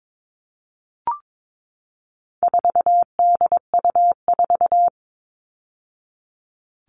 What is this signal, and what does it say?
Morse code, heard as a keyed tone, about 22 wpm, 700 Hz. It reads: 4DU4
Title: All Cut Numbers - 4-Digit 22wpm